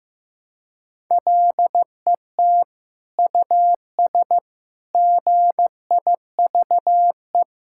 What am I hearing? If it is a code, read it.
Morse code, heard as a keyed tone, about 15 wpm, 700 Hz. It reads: LET US GIVE
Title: Sentences from Top 1000 Words - Review Part 5 - 15wpm